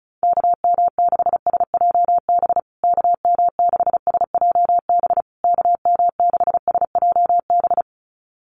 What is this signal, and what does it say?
Morse code, heard as a keyed tone, about 34 wpm, 700 Hz. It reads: KM6SJB KM6SJB KM6SJB
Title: USA Call Signs Repeated 3x 35wpm